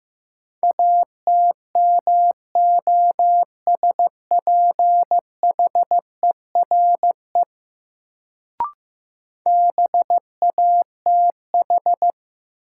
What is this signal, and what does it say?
Morse code, heard as a keyed tone, about 15 wpm, 700 Hz. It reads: ATMOSPHERE BATH
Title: Sentences from Top 1600 Words - New Word Drill Alt - 15wpm